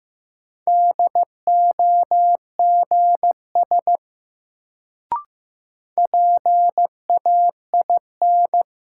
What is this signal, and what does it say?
Morse code, heard as a keyed tone, about 15 wpm, 700 Hz. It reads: DOGS PAIN
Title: Sentences from Top 1600 Words - New Word Drill - 15wpm